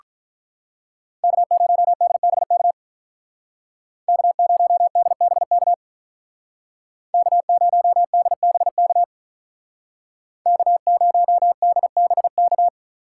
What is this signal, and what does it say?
Morse code, heard as a keyed tone, about 46 wpm, 700 Hz. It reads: K0DBK K0DBK K0DBK K0DBK